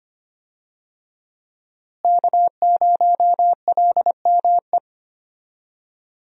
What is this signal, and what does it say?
Morse code, heard as a keyed tone, about 25 wpm, 700 Hz. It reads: K0LME
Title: USA Call Signs - 5 Characters - Repeated 2x - 25wpm